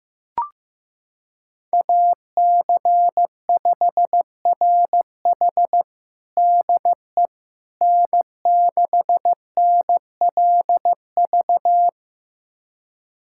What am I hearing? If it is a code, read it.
Morse code, heard as a keyed tone, about 16 wpm, 700 Hz. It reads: AC5RH DE N6NLV
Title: USA Callsign DE Callsign 15wpm